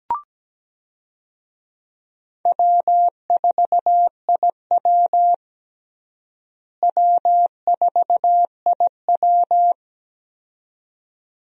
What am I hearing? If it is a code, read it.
Morse code, heard as a keyed tone, about 17 wpm, 700 Hz. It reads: W4IW W4IW
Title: USA Call Signs - 4 Characters - Repeated 2x - 17wpm